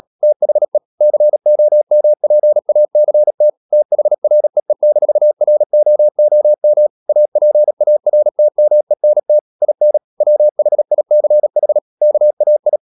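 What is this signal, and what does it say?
Morse code, heard as a keyed tone, about 37 wpm, 600 Hz. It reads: THE COMPACT THREE-ROOM APARTMENT IN WHICH KAI
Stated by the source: Morse code practice at 35–39 wpm sending English text of Ken Follett's novel Never, from CW med Gnister.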